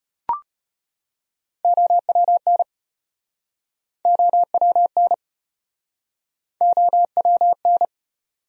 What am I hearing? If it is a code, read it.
Morse code, heard as a keyed tone, about 37 wpm, 700 Hz. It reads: OWN OWN OWN